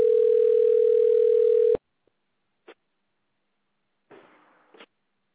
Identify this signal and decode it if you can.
Ringback tone (North America), 440 + 480 Hz.